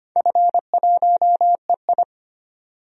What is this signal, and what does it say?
Morse code, heard as a keyed tone, about 25 wpm, 700 Hz. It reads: F1EI
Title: DXCC-100 Call Signs - 4 Characters - 25wpm